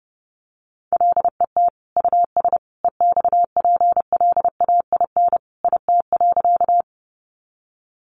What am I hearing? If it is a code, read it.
Morse code, heard as a keyed tone, about 30 wpm, 700 Hz. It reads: LET US EXPLAIN IT.
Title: Sentences from Top 1100 Words 30wpm